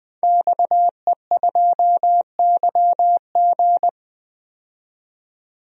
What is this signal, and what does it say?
Morse code, heard as a keyed tone, about 20 wpm, 700 Hz. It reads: XE2YG
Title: DXCC-100 Call Signs - 5 Characters - 20wpm